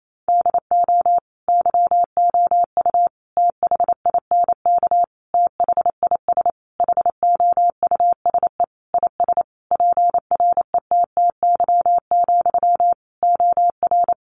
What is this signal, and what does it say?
Morse code, heard as a keyed tone, about 28 wpm, 700 Hz. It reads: DO YOU THINK THIS HOUSE IS PRETTY, OR
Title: Sentences from Top 1000 Words 28wpm